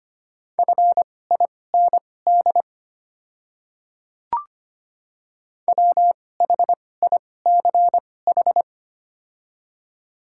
Morse code, heard as a keyed tone, about 25 wpm, spaced as for 18 wpm, 700 Hz. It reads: FIND WHICH
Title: 2x Character Spacing - Top 100 Words - 25wpm